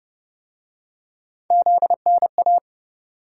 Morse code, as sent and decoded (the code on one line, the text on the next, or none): --.. -. .-
ZNA